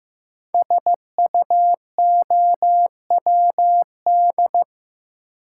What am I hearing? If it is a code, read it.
Morse code, heard as a keyed tone, about 15 wpm, 700 Hz. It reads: SUOWD
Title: Code Groups - 5 Letters - 15wpm